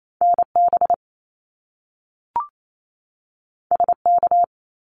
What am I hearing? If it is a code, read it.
Morse code, heard as a keyed tone, about 28 wpm, 700 Hz. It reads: NB SK
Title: Canadian Provinces and Territories 28wpm